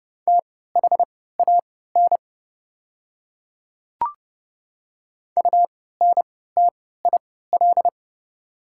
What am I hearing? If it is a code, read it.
Morse code, heard as a keyed tone, about 30 wpm, spaced as for 17 wpm, 700 Hz. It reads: THAN UNTIL
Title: 3x Character Spacing - Top 300 Words - 30wpm